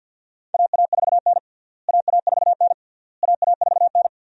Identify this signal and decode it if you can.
Morse code, heard as a keyed tone, about 53 wpm, 700 Hz. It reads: AA4N AA4N AA4N
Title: USA Call Signs - 4 Characters - Repeated 3x - 50wpm